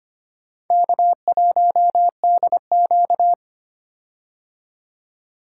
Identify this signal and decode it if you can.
Morse code, heard as a keyed tone, about 25 wpm, 700 Hz. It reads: K1DQ
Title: K1USN SST QSO Practice - 25wpm